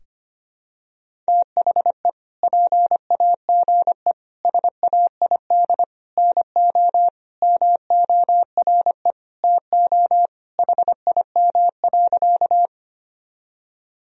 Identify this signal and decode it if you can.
Morse code, heard as a keyed tone, about 25 wpm, 700 Hz. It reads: THE PAGE SAID NO MORE TO HIM.
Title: Sentences from Top 1000 Words - Review Part 1 - 25wpm